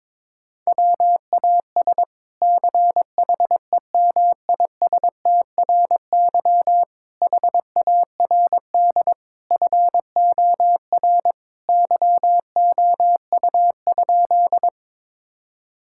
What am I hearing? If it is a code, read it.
Morse code, heard as a keyed tone, about 22 wpm, 700 Hz. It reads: WAS CHEMISTRY HARD FOR YOU?
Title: Sentences from Top 1000 Words - Review Part 6 - 22wpm